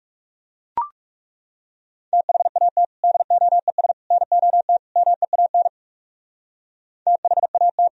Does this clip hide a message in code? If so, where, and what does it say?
Yes: Morse code from 2.1 to 7.9 s, heard as a keyed tone, about 38 wpm, 690 Hz. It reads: THAT DOES NOT MEAN THAT